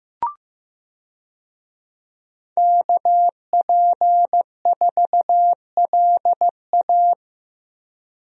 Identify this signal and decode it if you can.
Morse code, heard as a keyed tone, about 15 wpm, 700 Hz. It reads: KP4LA
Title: USA Call Signs - 5 Characters - Repeated 2x - 15wpm